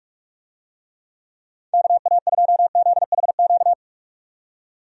Morse code, heard as a keyed tone, about 41 wpm, 700 Hz. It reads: KA2ZHQ
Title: USA Call Signs - 6 Characters - 45wpm